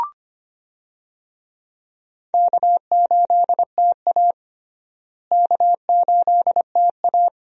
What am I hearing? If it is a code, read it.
Morse code, heard as a keyed tone, about 25 wpm, 700 Hz. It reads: K8TA K8TA
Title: USA Call Signs - 4 Characters - Repeated 2x - 25wpm